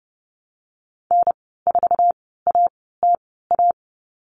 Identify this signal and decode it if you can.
Morse code, heard as a keyed tone, about 29 wpm, spaced as for 17 wpm, 700 Hz. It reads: N4ATA
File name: Lesson 15g - Number 5 - Call Signs 30wpm